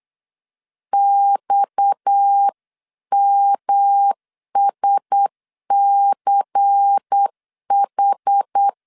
Morse code, heard as a keyed tone, about 8 wpm, 800 Hz. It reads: XMSCH
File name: LEZIONI / 6 PKX0 / 6 PKX 8wpm 3041